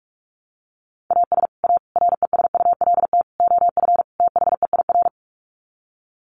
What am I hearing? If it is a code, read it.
Morse code, heard as a keyed tone, about 44 wpm, 700 Hz. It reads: AS A RESULT OF THEIR
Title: Sets of 5 Words Encore2 45wpm